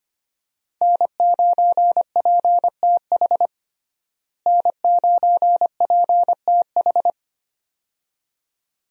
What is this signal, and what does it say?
Morse code, heard as a keyed tone, about 25 wpm, 700 Hz. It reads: N9PTH N9PTH
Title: USA Call Signs - 5 Characters - Repeated 2x - 25wpm